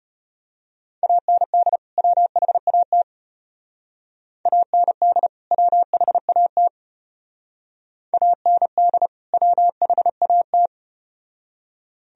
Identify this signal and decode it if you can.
Morse code, heard as a keyed tone, about 33 wpm, 700 Hz. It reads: AND WHAT AND WHAT AND WHAT